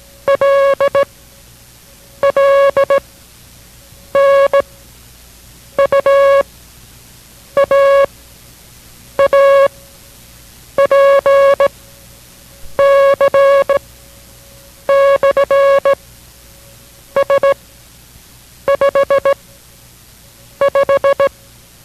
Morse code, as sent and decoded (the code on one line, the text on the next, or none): .-.. / .-.. / -. / ..- / .- / .- / .--. / -.-. / -..-. / ... / ..... / .....
L L N U A A P C / S 5 5